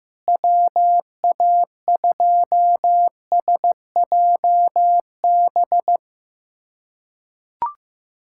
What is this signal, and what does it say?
Morse code, heard as a keyed tone, about 15 wpm, 700 Hz. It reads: WA2SJB
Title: USA Callsign DE Callsign 15wpm